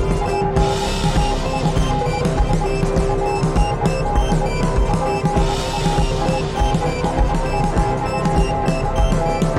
Morse code, heard as a keyed tone, about 20 wpm, 820 Hz. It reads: PAINS AND PAINS, IT